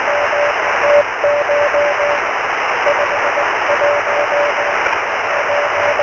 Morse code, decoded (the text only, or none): none